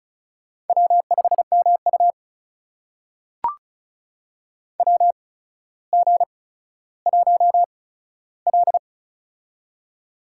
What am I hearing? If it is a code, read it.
Morse code, heard as a keyed tone, about 36 wpm, spaced as for 15 wpm, 700 Hz. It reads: W5MU WG1L
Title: USA Call Signs - 4 Characters - 8x Character Spacing - 35wpm